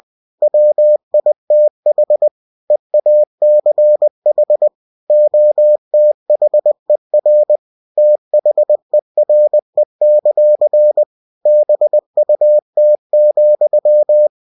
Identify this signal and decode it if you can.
Morse code, heard as a keyed tone, about 20 wpm, 600 Hz. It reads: WITH EACH OTHER THERE; BUT,
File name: cw-books / winnie-the-pooh / winnie-20wpm-page-0024